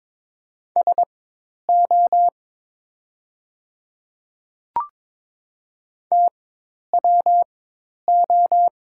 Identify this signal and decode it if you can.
Morse code, heard as a keyed tone, about 23 wpm, spaced as for 10 wpm, 700 Hz. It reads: SO TWO
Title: 4x Character Spacing - Top 500 Words - 22wpm